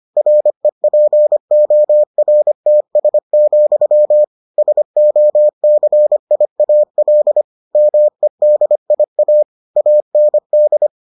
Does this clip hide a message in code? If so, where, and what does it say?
Yes: Morse code from 0.2 to 10.9 s, heard as a keyed tone, about 25 wpm, 600 Hz. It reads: REPORTS, SOCIAL MEDIA AND